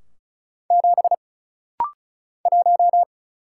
Morse code, heard as a keyed tone, about 36 wpm, 700 Hz. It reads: 7 1
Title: Single Letter-Number - Rapid-Fire 35wpm